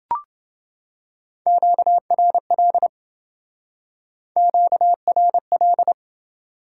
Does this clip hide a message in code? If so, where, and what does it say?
Yes: Morse code from 1.5 to 5.9 s, heard as a keyed tone, about 27 wpm, 700 Hz. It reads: QRL QRL